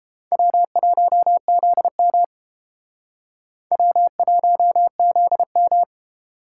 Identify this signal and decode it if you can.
Morse code, heard as a keyed tone, about 29 wpm, 700 Hz. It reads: W1ZM W1ZM